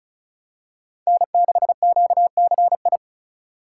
Morse code, heard as a keyed tone, about 36 wpm, 700 Hz. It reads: N6QCI
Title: USA Call Signs - 5 Characters - 4x Character Spacing - 35wpm